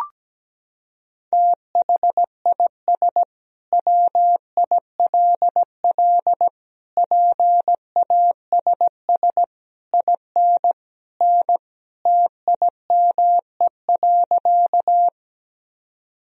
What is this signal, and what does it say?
Morse code, heard as a keyed tone, about 17 wpm, 700 Hz. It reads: THIS WILL PASS IN N TIME.
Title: Sentences from Top 400 Words and Abbreviations 17wpm